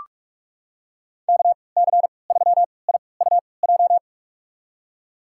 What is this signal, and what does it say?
Morse code, heard as a keyed tone, about 46 wpm, spaced as for 25 wpm, 700 Hz. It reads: KC3IUJ
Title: USA Call Signs - 6 Characters - 3x Character Spacing - 45wpm